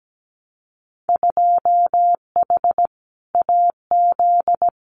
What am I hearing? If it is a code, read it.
Morse code, heard as a keyed tone, about 17 wpm, 700 Hz. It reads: 2H AZ
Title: Winter Field Day Exchanges 17wpm